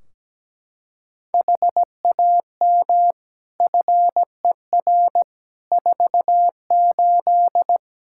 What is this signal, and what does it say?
Morse code, heard as a keyed tone, about 17 wpm, 700 Hz. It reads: HAM FER 48
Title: Practice QSO - Exchange 3 - More Info - 17wpm